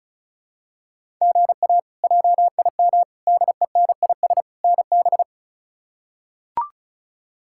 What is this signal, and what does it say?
Morse code, heard as a keyed tone, about 36 wpm, 700 Hz. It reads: GA JIM DENIS NB
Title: K1USN SST QSO Practice - 35wpm